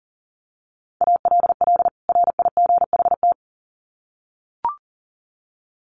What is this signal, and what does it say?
Morse code, heard as a keyed tone, about 40 wpm, 700 Hz. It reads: ALL RIGHT
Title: Sets of 2 Words 40wpm